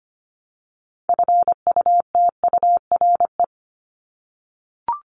Morse code, heard as a keyed tone, about 25 wpm, 700 Hz. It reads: FUTURE
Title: Top 501-750 Words 25wpm